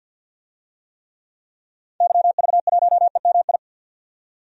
Morse code, heard as a keyed tone, about 52 wpm, 700 Hz. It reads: YV1EMI
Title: DXCC-100 Call Signs - 6 Characters - 50wpm